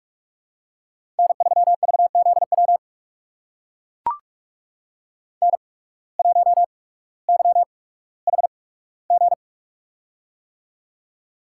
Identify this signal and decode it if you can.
Morse code, heard as a keyed tone, about 44 wpm, spaced as for 19 wpm, 700 Hz. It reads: N3VZW N1YHG
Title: USA Call Signs - 5 Characters - 8x Character Spacing - 45wpm